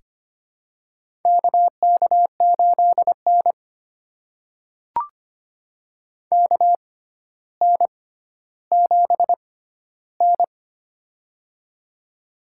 Morse code, heard as a keyed tone, about 25 wpm, spaced as for 13 wpm, 700 Hz. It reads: KK8N KN7N
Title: USA Call Signs - 4 Characters - 6x Character Spacing - 25wpm